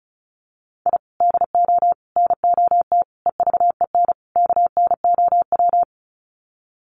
Morse code, heard as a keyed tone, about 34 wpm, 700 Hz. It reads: I DO NOT EVEN KNOW
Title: Sets of 5 Words 35wpm